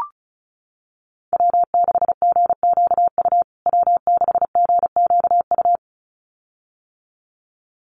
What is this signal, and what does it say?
Morse code, heard as a keyed tone, about 34 wpm, 700 Hz. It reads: W6GQU W6GQU
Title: USA Call Signs Repeated 2x 35wpm